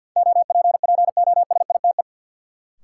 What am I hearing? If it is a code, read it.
Morse code, heard as a keyed tone, about 52 wpm, 700 Hz. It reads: OPPOSITE
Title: Sentences from Top 1300 Words - New Word Drill - 50wpm